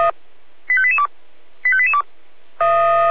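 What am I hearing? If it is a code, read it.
Five-tone selective call (ZVEI), heard as quick runs of five tones, tones of about 70 ms. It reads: 85882 85882
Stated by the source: ZVEI five-tone call received by SDR.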